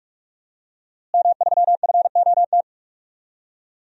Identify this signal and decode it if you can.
Morse code, heard as a keyed tone, about 48 wpm, 700 Hz. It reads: M3FOT
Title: DXCC-100 Call Signs - 5 Characters - Repeated 3x - 45wpm